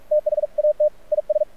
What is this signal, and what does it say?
Morse code, heard as a keyed tone, about 46 wpm, 600 Hz. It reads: THAT IS
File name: archived / CRSnet